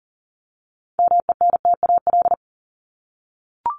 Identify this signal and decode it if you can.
Morse code, heard as a keyed tone, about 40 wpm, 700 Hz. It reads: MENTAL
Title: Top 501-750 Words 40wpm